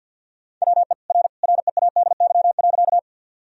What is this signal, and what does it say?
Morse code, heard as a keyed tone, about 50 wpm, 700 Hz. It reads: WE R READY.
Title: Sentences from Top 400 Words and Abbreviations 50wpm